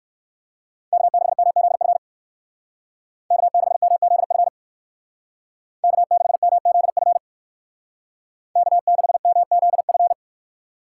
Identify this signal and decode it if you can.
Morse code, heard as a keyed tone, about 63 wpm, 700 Hz. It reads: K6MZF K6MZF K6MZF K6MZF